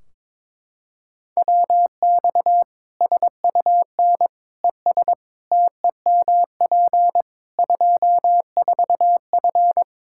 Morse code, heard as a keyed tone, about 22 wpm, 700 Hz. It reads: WX SUN ES TEMP 24F
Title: Practice QSO - Exchange 2 - Info - 22wpm